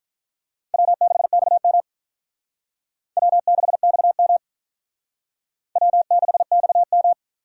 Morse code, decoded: W6XM W6XM W6XM

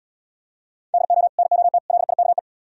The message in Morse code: DO NOT LIKE

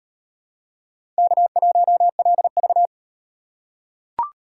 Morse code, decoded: K1LV

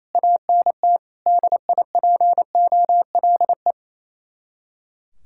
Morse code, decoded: ANT DIPOLE

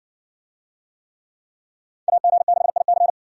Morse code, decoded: AG6IC

